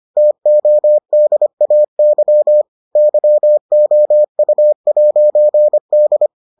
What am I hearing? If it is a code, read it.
Morse code, heard as a keyed tone, about 25 wpm, 600 Hz. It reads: TODAY YOU'D